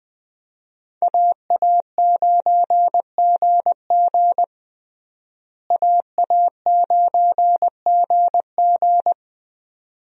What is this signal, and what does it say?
Morse code, heard as a keyed tone, about 21 wpm, 700 Hz. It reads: AA9GG AA9GG
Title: USA Call Signs - 5 Characters - Repeated 2x - 20wpm